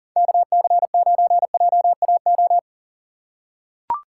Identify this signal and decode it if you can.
Morse code, heard as a keyed tone, about 40 wpm, 700 Hz. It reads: KC9JAO
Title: USA Call Signs - 6 Characters - 3x Character Spacing - 40wpm